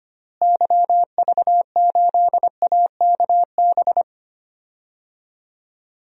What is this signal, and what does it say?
Morse code, heard as a keyed tone, about 25 wpm, 700 Hz. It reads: YV8AKB
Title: DXCC-100 Call Signs - 6 Characters - Repeated 3x - 25wpm